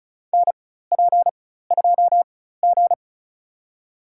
Morse code, heard as a keyed tone, about 36 wpm, spaced as for 16 wpm, 700 Hz. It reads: NP2G
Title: USA Call Signs - 4 Characters - 4x Character Spacing - 35wpm